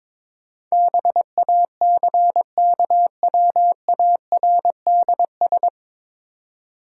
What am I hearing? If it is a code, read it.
Morse code, heard as a keyed tone, about 22 wpm, 700 Hz. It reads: BACKWARDS
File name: 9 Letter Words - First 3 Letters Given - 22wpm